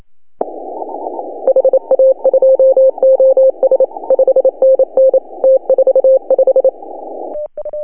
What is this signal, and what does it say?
Morse code, heard as a keyed tone, about 27 wpm, 530 Hz. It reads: HA2OS 5NN T45
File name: CW / HA2OS RHOB 2017 CW